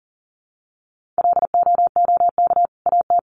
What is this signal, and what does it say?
Morse code, heard as a keyed tone, about 40 wpm, 700 Hz. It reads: LOOK AT
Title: Sentences from Top 1500 Words 40wpm